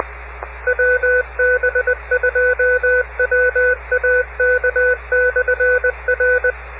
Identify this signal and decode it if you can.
Morse code, heard as a keyed tone, about 20 wpm, 1500 Hz, with radio noise. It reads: WB2WAK/R